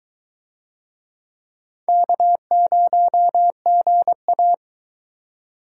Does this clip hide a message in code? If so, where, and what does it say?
Yes: Morse code from 1.9 to 4.5 s, heard as a keyed tone, about 22 wpm, 700 Hz. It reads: K0GA